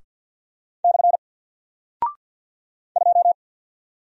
Morse code, heard as a keyed tone, about 48 wpm, 700 Hz. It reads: / 2
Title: Single Character (Letter:Number:Punctuation:Prosign) 50wpm